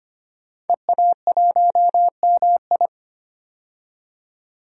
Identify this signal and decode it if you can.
Morse code, heard as a keyed tone, about 25 wpm, 700 Hz. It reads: EA1MI
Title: DXCC-100 Call Signs - 5 Characters - 25wpm